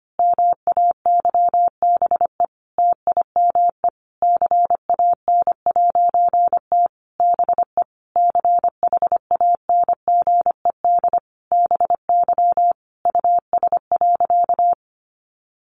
Morse code, heard as a keyed tone, about 25 wpm, 700 Hz. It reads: MAYBE TIME CAN'T BE CHANGED BY US.
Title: A Sound Of Thunder 25wpm